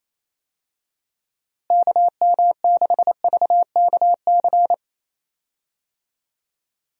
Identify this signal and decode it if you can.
Morse code, heard as a keyed tone, about 27 wpm, 700 Hz. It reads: KM6VKC
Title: USA Call Signs - 6 Characters - Repeated 2x - 28wpm